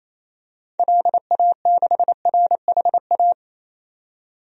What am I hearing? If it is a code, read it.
Morse code, heard as a keyed tone, about 27 wpm, 700 Hz. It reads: LA6RHA
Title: DXCC-100 Call Signs - 6 Characters - 28wpm